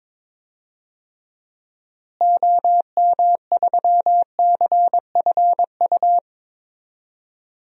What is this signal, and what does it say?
Morse code, heard as a keyed tone, about 22 wpm, 700 Hz. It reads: OM3CFU
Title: DXCC-100 Call Signs - 6 Characters - Repeated 3x - 22wpm